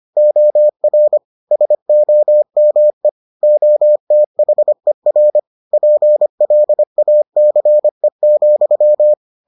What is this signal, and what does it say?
Morse code, heard as a keyed tone, about 25 wpm, 600 Hz. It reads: OR SOME OTHER PLACE,